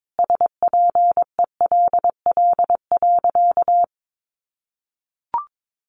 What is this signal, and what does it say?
Morse code, heard as a keyed tone, about 22 wpm, 700 Hz. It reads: SPELL.
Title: Sentences from Top 1200 Words 22wpm